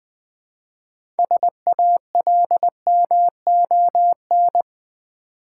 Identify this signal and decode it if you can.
Morse code, heard as a keyed tone, about 20 wpm, 690 Hz. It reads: SALMON
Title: Easy Words with Hints - 20wpm